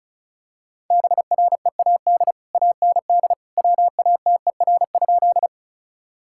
Morse code, heard as a keyed tone, about 34 wpm, 700 Hz. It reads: BREAD AND WATER?
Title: Sentences from Top 1600 Words 35wpm